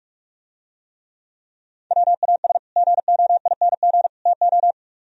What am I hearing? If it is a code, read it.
Morse code, heard as a keyed tone, about 44 wpm, 700 Hz. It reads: WAS GOING TO